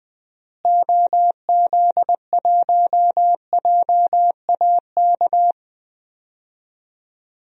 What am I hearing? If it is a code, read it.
Morse code, heard as a keyed tone, about 20 wpm, 700 Hz. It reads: OZ1JAK
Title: DXCC-100 Call Signs - 6 Characters - Repeated 2x - 20wpm